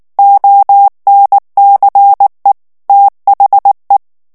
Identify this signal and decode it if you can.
Morse code, heard as a keyed tone, about 19 wpm, 800 Hz. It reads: ONCE THE
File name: ANGLAIS texte clair 3 vitesse 18 mots minute